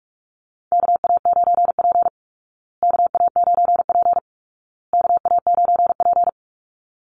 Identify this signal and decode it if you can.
Morse code, heard as a keyed tone, about 44 wpm, 700 Hz. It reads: KA9P KA9P KA9P
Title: Call Signs Repeated 2x - 45wpm